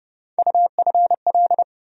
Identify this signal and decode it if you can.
Morse code, heard as a keyed tone, about 30 wpm, 700 Hz. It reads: UFL
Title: USA Call Sign Postfixes - 3 Letters 30wpm